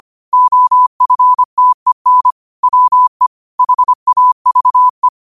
Morse code, heard as a keyed tone, about 25 wpm, 1000 Hz. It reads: OFTEN WE HAVE
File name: qotdfiles / 25wpm